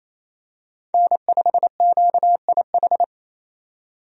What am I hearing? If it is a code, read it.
Morse code, heard as a keyed tone, about 27 wpm, 700 Hz. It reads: N5QIH
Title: USA Call Signs - 5 Characters - 28wpm